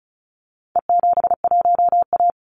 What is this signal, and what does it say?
Morse code, heard as a keyed tone, about 34 wpm, 700 Hz. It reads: E71A
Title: Call Signs - 35wpm